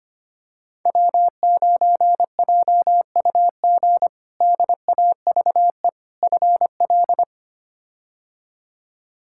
Morse code, heard as a keyed tone, about 25 wpm, 700 Hz. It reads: W9JUG DAVE FL
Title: K1USN SST QSO Practice - 25wpm